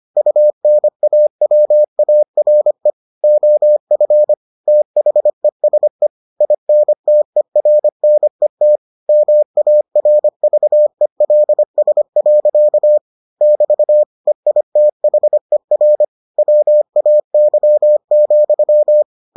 Morse code, heard as a keyed tone, about 25 wpm, 600 Hz. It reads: UNAWARE OF THESE INTERNET MARVELS. <BT> EITHER WAY,